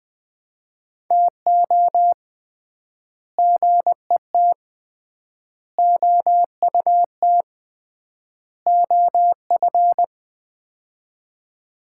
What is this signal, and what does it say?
Morse code, heard as a keyed tone, about 20 wpm, 700 Hz. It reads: TO GET OUT OF
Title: 3x Word-Spacing - Sets of 4 Words - 20wpm